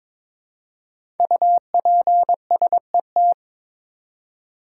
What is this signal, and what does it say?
Morse code, heard as a keyed tone, about 22 wpm, 700 Hz. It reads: UPSET
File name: 5 Letter Words - First 3 Letters Given - 22wpm